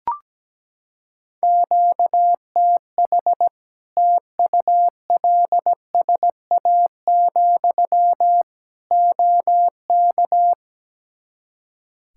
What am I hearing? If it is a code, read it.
Morse code, heard as a keyed tone, about 17 wpm, 700 Hz. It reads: QTH TULSA, OK
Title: Practice QSO - Exchange 1 - Beginning - 17wpm